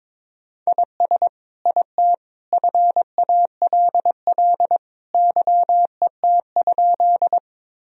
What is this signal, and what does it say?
Morse code, heard as a keyed tone, about 22 wpm, 700 Hz. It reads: IS IT FALL YET?
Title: Sentences from Top 1000 Words - Review Part 2 - 22wpm